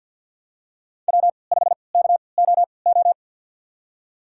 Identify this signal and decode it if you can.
Morse code, heard as a keyed tone, about 50 wpm, spaced as for 28 wpm, 700 Hz. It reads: W5KOO
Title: USA Call Signs - 5 Characters - 3x Character Spacing - 50wpm